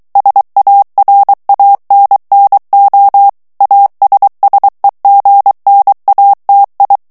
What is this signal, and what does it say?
Morse code, heard as a keyed tone, about 23 wpm, 800 Hz. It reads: SARANNO ASSEGNATI
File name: ITALIEN texte clair 2 vitesse 22 mots minute